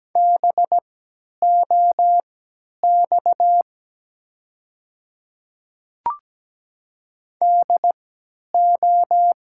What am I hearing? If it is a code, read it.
Morse code, heard as a keyed tone, about 17 wpm, spaced as for 10 wpm, 700 Hz. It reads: BOX DO
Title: 3x Character Spacing - Top 500 Words - 17wpm